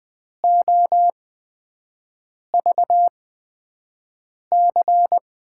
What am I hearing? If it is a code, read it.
Morse code, heard as a keyed tone, about 20 wpm, spaced as for 5 wpm, 700 Hz. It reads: OVC
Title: USA Call Signs - 6 Characters - 8x Character Spacing - 20wpm